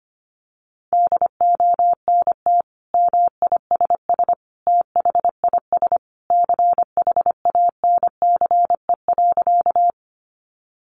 Morse code, heard as a keyed tone, about 25 wpm, 700 Hz. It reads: DONT MISS THIS CHANCE.
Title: Sentences from Top 800 Words 25wpm